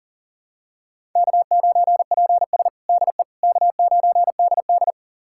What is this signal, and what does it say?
Morse code, heard as a keyed tone, about 40 wpm, 700 Hz. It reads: K9PS DE K9DD
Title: USA Callsign DE Callsign - 4 Characters - 40wpm